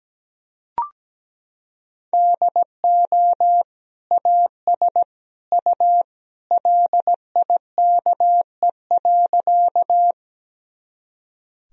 Morse code, heard as a keyed tone, about 17 wpm, 700 Hz. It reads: DO AS U LIKE.
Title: Sentences from Top 100 Words and Abbreviations 17wpm